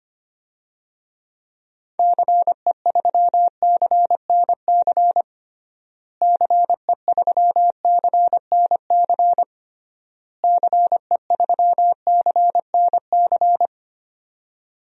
Morse code, heard as a keyed tone, about 25 wpm, 700 Hz. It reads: CE3CNC CE3CNC CE3CNC
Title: DXCC-100 Call Signs - 6 Characters - Repeated 3x - 25wpm